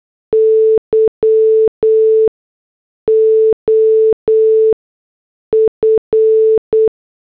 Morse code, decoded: Y OF